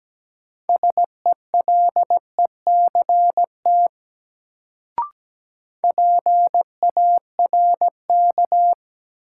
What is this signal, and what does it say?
Morse code, heard as a keyed tone, about 17 wpm, 700 Hz. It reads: SELECT PARK
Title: Sentences from Top 1300 Words - New Word Drill - 17wpm